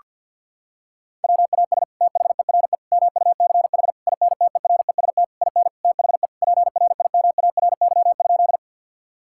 Morse code, heard as a keyed tone, about 50 wpm, 700 Hz. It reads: WAS THERE MUCH INTEREST IN THE PRIMARY?
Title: Sentences from Top 1000 Words - Review Part 5 - 50wpm